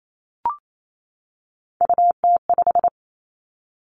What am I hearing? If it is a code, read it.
Morse code, heard as a keyed tone, about 28 wpm, 700 Hz. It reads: UT5
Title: Callsign Prefixes for Top 25 DXCC - 28wpm